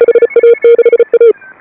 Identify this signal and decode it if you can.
Morse code, heard as a keyed tone, about 35 wpm, 470 Hz. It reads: HA6A
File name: AUDIO2018 / HA5BA CW